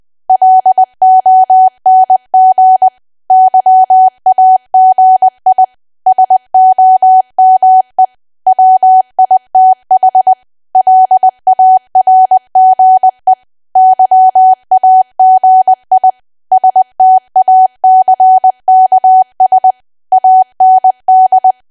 Morse code, heard as a keyed tone, about 20 wpm, 750 Hz. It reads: LONG YAGI SOME WITH LARGE YAGI STACKS AND
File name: cwa / W1AW-Level3 / 20WPM